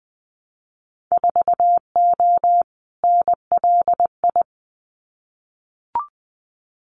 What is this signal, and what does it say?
Morse code, heard as a keyed tone, about 20 wpm, 700 Hz. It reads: 4O NLI
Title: Winter Field Day Exchanges Encore 20wpm